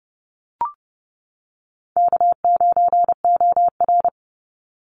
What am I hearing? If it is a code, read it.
Morse code, heard as a keyed tone, about 30 wpm, 700 Hz. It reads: K9OR
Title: Call Signs - 30wpm